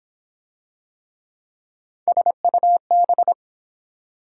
Morse code, heard as a keyed tone, about 23 wpm, 700 Hz. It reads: SUB